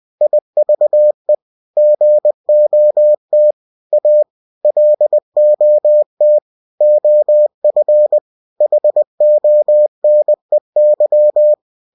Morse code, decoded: IVE GOT A LOT OF HONEY

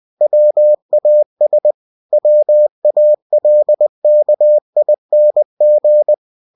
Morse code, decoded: WAS WALKING

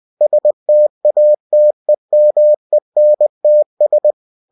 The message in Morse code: STATEMENTS

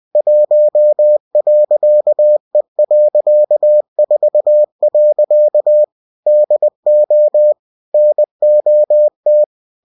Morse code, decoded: 1.E.4. DO NOT

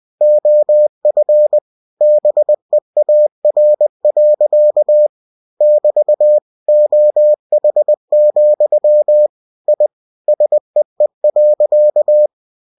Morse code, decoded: OF BEAR. <BT> OH, I SEE.

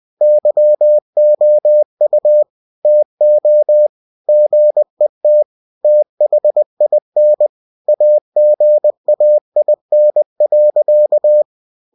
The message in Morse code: YOU TO GET THIN AGAIN.

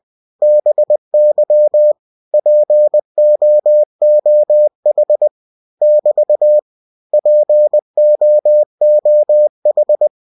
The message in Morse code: BY POOH <BT> POOH

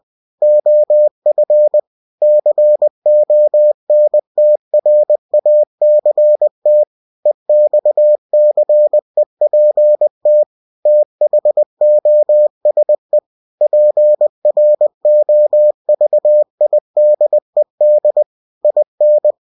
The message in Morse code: OF CONTRACT EXCEPT THOSE PROVIDED IN

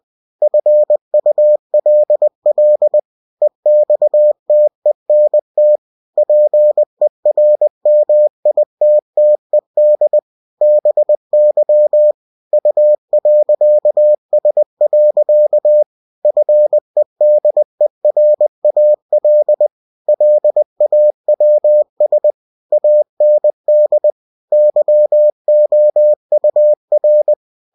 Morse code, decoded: FULL EXTENT PERMITTED BY U.S. FEDERAL LAWS AND YOUR

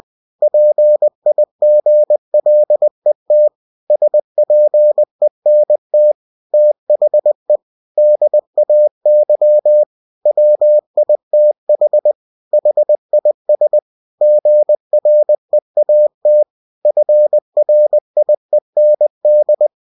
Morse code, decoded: PIGLET SPENT THE DAY WITH HIS GREAT FRIEND

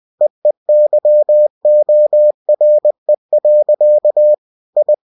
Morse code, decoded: EEYORE. I

